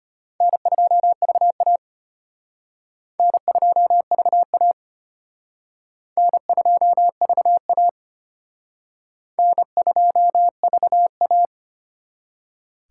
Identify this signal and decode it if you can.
Morse code, heard as a keyed tone, about 33 wpm, 700 Hz. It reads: N2VA N2VA N2VA N2VA